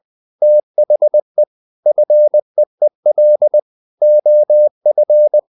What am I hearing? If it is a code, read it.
Morse code, heard as a keyed tone, about 20 wpm, 600 Hz. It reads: THE FEEL OF
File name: the old man and the sea part 2 20wpm